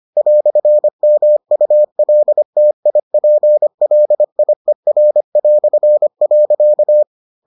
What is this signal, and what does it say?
Morse code, heard as a keyed tone, about 25 wpm, 600 Hz. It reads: "MULTIPLIER".